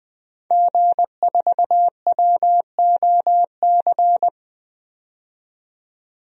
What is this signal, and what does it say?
Morse code, heard as a keyed tone, about 20 wpm, 700 Hz. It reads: G4WOC
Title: DXCC-100 Call Signs - 5 Characters - 20wpm